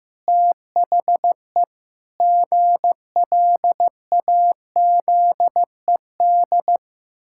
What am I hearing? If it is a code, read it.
Morse code, heard as a keyed tone, about 14 wpm, 700 Hz. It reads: THE GLAZED
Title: Pangram Challenge v2 - 15wpm